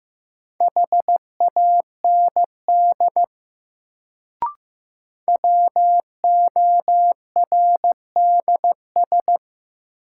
Morse code, heard as a keyed tone, about 15 wpm, 700 Hz. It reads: HAND WORDS
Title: Sets of 3 Words - New Word Drill - 15wpm